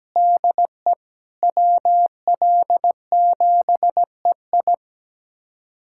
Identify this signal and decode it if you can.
Morse code, heard as a keyed tone, about 17 wpm, 700 Hz. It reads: DE WL7EI
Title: USA Callsign DE Callsign - 5 Characters - 17wpm